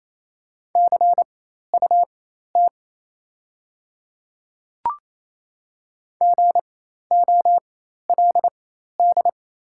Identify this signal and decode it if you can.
Morse code, heard as a keyed tone, about 28 wpm, spaced as for 13 wpm, 700 Hz. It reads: CUT GOLD
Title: 4x Character Spacing - Top 500 Words - 28wpm